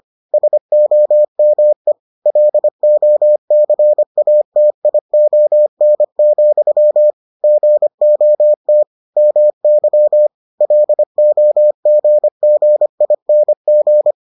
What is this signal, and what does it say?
Morse code, heard as a keyed tone, about 25 wpm, 600 Hz. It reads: SOME LOCATION, GOT MY LOGGING